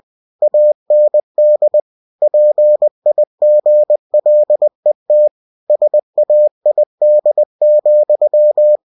Morse code, heard as a keyed tone, about 20 wpm, 600 Hz. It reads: AND PIGLET SAID,
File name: cw-books / winnie-the-pooh / winnie-20wpm-page-0034